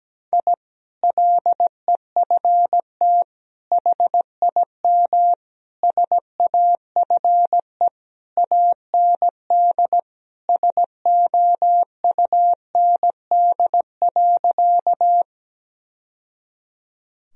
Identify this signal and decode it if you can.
Morse code, heard as a keyed tone, about 17 wpm, 700 Hz. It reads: I LEFT HIM SAFE AND SOUND.
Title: Sentences from Top 1000 Words - Review Part 1 - 17wpm